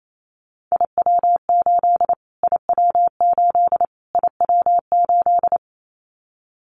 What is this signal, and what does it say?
Morse code, heard as a keyed tone, about 28 wpm, 700 Hz. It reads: IW8 IW8 IW8
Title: Callsign Prefixes for Top 25 DXCC - Repeated 3x - 28wpm